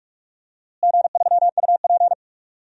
Morse code, heard as a keyed tone, about 42 wpm, 700 Hz. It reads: G3UP
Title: DXCC-100 Call Signs - 4 Characters - Repeated 2x - 45wpm